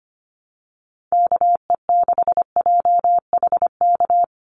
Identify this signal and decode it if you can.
Morse code, heard as a keyed tone, about 25 wpm, 690 Hz. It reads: KE6JHK
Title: USA Call Signs 25wpm